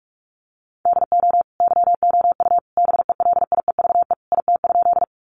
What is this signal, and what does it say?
Morse code, heard as a keyed tone, about 44 wpm, 700 Hz. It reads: DO YOU BELIEVE IT?
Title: Sentences from Top 1000 Words 45wpm